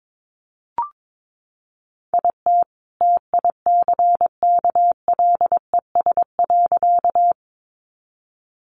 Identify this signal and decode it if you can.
Morse code, heard as a keyed tone, about 22 wpm, 700 Hz. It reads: IT TICKLES.
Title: Additional Easy Phrases 22wpm